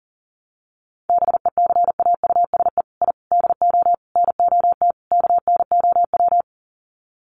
Morse code, heard as a keyed tone, about 40 wpm, 700 Hz. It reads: BECAUSE I DO NOT KNOW
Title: Sets of 5 Words Encore2 40wpm